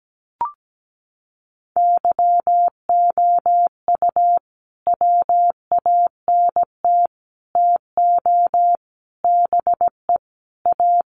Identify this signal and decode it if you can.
Morse code, heard as a keyed tone, about 17 wpm, 700 Hz. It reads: YOU WANT TO BE A
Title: Sets of 5 Words Encore2 17wpm